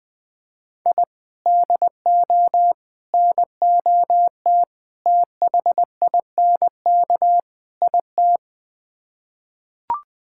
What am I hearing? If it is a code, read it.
Morse code, heard as a keyed tone, about 20 wpm, 700 Hz. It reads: I DO NOT THINK IT
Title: Build A Phrase - 5-Word Phrases - 20wpm